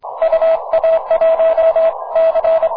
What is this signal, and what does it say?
Morse code, heard as a keyed tone, about 30 wpm, spaced as for 24 wpm, 690 Hz. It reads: UA1C
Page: На 15 метров, в лучшем виде - прекрасный сигнал!